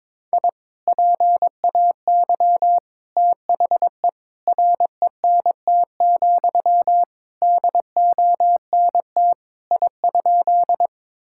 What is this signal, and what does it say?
Morse code, heard as a keyed tone, about 22 wpm, 700 Hz. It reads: I PAY THE RENT, DONT I?
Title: Sentences from Top 1000 Words - Review Part 5 - 22wpm